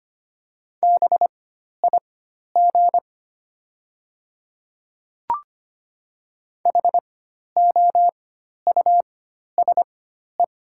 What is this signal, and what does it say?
Morse code, heard as a keyed tone, about 25 wpm, spaced as for 12 wpm, 700 Hz. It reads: BIG HOUSE
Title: 4x Character Spacing - Top 300 Words - 25wpm